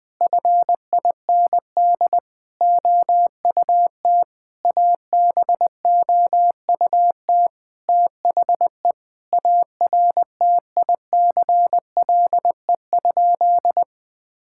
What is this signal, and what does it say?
Morse code, heard as a keyed tone, about 20 wpm, 700 Hz. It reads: FIND OUT ABOUT THE ARTICLE?
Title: Sentences from Top 1000 Words - Review Part 5 - 20wpm